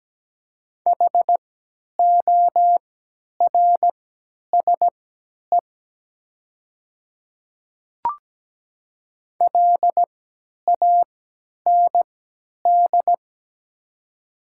Morse code, heard as a keyed tone, about 17 wpm, spaced as for 10 wpm, 700 Hz. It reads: HORSE LAND
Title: 3x Character Spacing - Top 300 Words - 17wpm